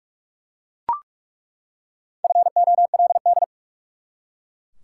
Morse code, heard as a keyed tone, about 47 wpm, 700 Hz. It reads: FOLD
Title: Four Letter Words - 45wpm